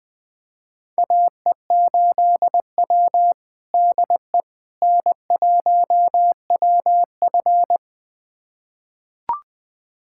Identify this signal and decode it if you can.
Morse code, heard as a keyed tone, about 20 wpm, 700 Hz. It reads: AE8W DE N1WF
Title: USA Callsign DE Callsign - 4 Characters - 20wpm